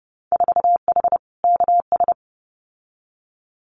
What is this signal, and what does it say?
Morse code, heard as a keyed tone, about 30 wpm, 700 Hz. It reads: 4H KS
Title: Winter Field Day Exchanges Encore 30wpm